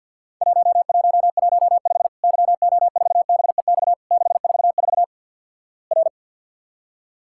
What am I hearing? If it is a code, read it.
Morse code, heard as a keyed tone, about 48 wpm, 700 Hz. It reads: 1115 YO4BEX 644 R
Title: Worked All Europe DX Contest - QTC Exchange - 50wpm